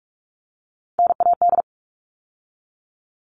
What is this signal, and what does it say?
Morse code, heard as a keyed tone, about 44 wpm, 700 Hz. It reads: NAD
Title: All Cut Numbers - 3-Digit 45wpm